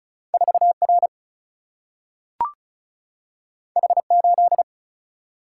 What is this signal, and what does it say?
Morse code, heard as a keyed tone, about 36 wpm, 700 Hz. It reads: 4R H8
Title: Single-Letter Single-Number 35wpm